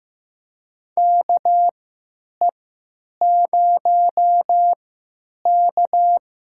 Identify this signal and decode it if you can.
Morse code, heard as a keyed tone, about 15 wpm, spaced as for 9 wpm, 700 Hz. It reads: KE0K
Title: USA Call Signs - 3x Character Spacing - 15wpm